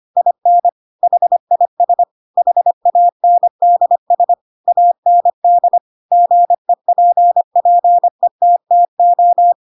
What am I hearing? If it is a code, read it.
Morse code, heard as a keyed tone, about 25 wpm, 700 Hz. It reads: IN HIS HANDS AND GEPPETTO
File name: Pinocchio / Pinocchio ch2 25 25 ik1hge